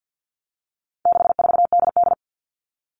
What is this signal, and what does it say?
Morse code, heard as a keyed tone, about 50 wpm, 700 Hz. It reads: 64DD
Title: All Cut Numbers - 4-Digit 50wpm